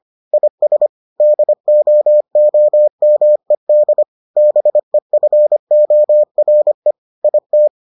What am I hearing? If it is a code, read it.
Morse code, heard as a keyed tone, about 25 wpm, 600 Hz. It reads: IS DOOMED BEFORE IT